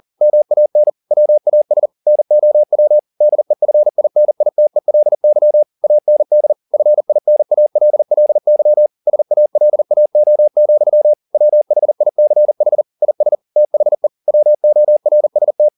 Morse code, heard as a keyed tone, about 40 wpm, 600 Hz. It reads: MAN WAS NOW DEFINITELY AND FINALLY SALAO, WHICH IS THE WORST
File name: the old man and the sea 1 40wpm